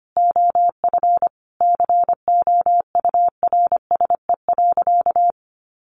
Morse code, heard as a keyed tone, about 25 wpm, 700 Hz. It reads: OF COURSE.
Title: Sentences from Top 1100 Words 25wpm